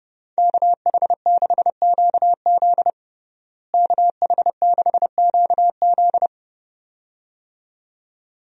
Morse code, heard as a keyed tone, about 30 wpm, 700 Hz. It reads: KH6QZ KH6QZ
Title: USA Call Signs - 5 Characters - Repeated 2x - 30wpm